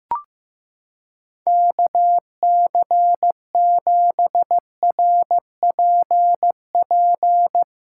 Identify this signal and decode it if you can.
Morse code, heard as a keyed tone, about 16 wpm, 700 Hz. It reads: KC7RPP
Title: USA Callsign DE Callsign - 6 Characters - 15wpm